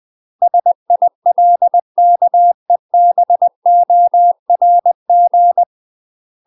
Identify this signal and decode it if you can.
Morse code, heard as a keyed tone, about 20 wpm, 700 Hz. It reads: SILKEBORG
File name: UK MASTER QSOs TRAINING/20WPM